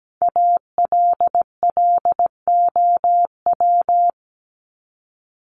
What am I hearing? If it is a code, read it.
Morse code, heard as a keyed tone, about 17 wpm, 700 Hz. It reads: ALLOW
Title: Top 501-750 Words 17wpm